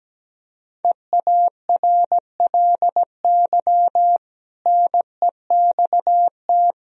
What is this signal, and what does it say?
Morse code, heard as a keyed tone, about 17 wpm, 700 Hz. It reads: EARLY NEXT
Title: Sentences from Top 1000 Words - Review Part 4 - 17wpm